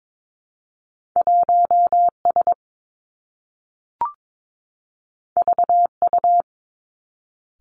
Morse code, heard as a keyed tone, about 22 wpm, 700 Hz. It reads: 1 S V U
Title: Two Challenging Letter-Number Combinations 22wpm